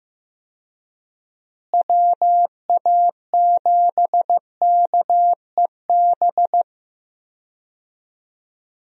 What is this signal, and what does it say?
Morse code, heard as a keyed tone, about 15 wpm, 700 Hz. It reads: WA7KEB
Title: USA Call Signs - 6 Characters - Repeated 2x - 15wpm